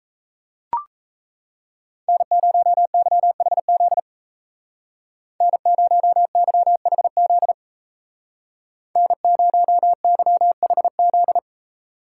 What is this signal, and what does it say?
Morse code, heard as a keyed tone, about 36 wpm, 700 Hz. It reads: N0YHZ N0YHZ N0YHZ